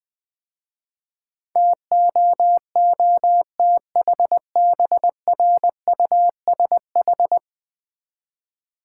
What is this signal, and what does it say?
Morse code, heard as a keyed tone, about 20 wpm, 700 Hz. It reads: TOOTHBRUSH
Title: Easy Words - Speed-Racing with Hints - 20wpm